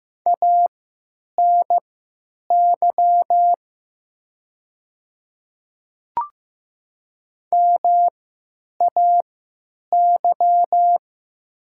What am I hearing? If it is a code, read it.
Morse code, heard as a keyed tone, about 15 wpm, spaced as for 9 wpm, 700 Hz. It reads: ANY MAY
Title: 3x Character Spacing - Top 500 Words - 15wpm